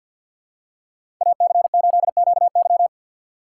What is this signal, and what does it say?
Morse code, heard as a keyed tone, about 53 wpm, 700 Hz. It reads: AC8QY
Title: USA Call Signs - 5 Characters - 4x Character Spacing - 50wpm